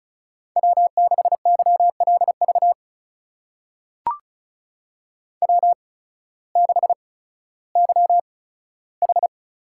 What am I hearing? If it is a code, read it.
Morse code, heard as a keyed tone, about 36 wpm, spaced as for 16 wpm, 700 Hz. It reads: W6YLV W6YH
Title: USA Call Signs - 5 Characters - 8x Character Spacing - 35wpm